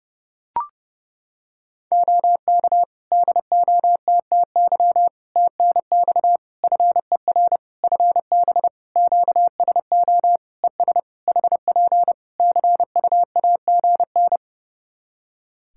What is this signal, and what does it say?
Morse code, heard as a keyed tone, about 30 wpm, 700 Hz. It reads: OK DOTTY TNX FER FB QSO ES HP CUAGN
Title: K1USN SST QSO Practice 30wpm